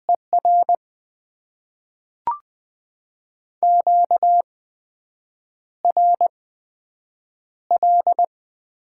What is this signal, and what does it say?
Morse code, heard as a keyed tone, about 20 wpm, spaced as for 7 wpm, 700 Hz. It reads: ER QRL
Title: Top 20 QSO Elements - 8x Character Spacing - 20wpm